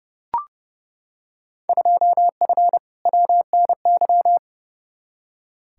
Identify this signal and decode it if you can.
Morse code, heard as a keyed tone, about 30 wpm, 700 Hz. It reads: 2F WNY
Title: Field Day Exchanges Encore - 30wpm